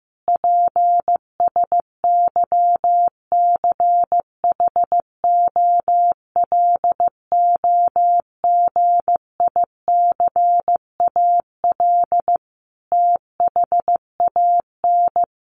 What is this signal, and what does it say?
Morse code, heard as a keyed tone, about 15 wpm, 700 Hz. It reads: PSYCHOLOGICAL THAN
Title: Sentences from Top 1400 Words 15wpm